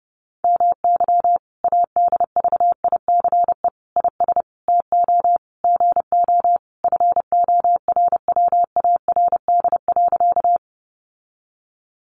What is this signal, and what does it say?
Morse code, heard as a keyed tone, about 28 wpm, 700 Hz. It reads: MY ADVICE IS TO GO FORWARD.
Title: Sentences from Top 1300 Words 30wpm